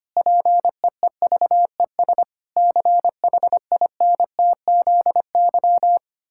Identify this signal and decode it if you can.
Morse code, heard as a keyed tone, about 25 wpm, 700 Hz. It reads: PEEVES CHINTZY
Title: Pangram Challenge v2 - 25wpm